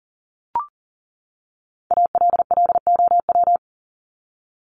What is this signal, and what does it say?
Morse code, heard as a keyed tone, about 40 wpm, 700 Hz. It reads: ALLOW
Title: ED Suffix - Top 100 Words with Base Words 40wpm